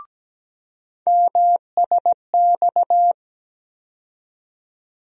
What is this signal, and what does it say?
Morse code, heard as a keyed tone, about 17 wpm, 700 Hz. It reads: MSX
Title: USA Call Sign Postfixes - 3 Letters 17wpm